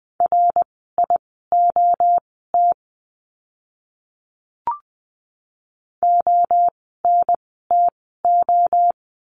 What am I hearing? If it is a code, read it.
Morse code, heard as a keyed tone, about 20 wpm, spaced as for 14 wpm, 700 Hz. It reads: RIOT ONTO
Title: 061 Lesson 11f - Letter h - Words 20wpm